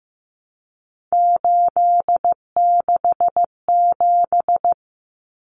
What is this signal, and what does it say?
Morse code, heard as a keyed tone, about 15 wpm, 700 Hz. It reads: 867
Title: Common Cut Numbers - 3-Digit 15wpm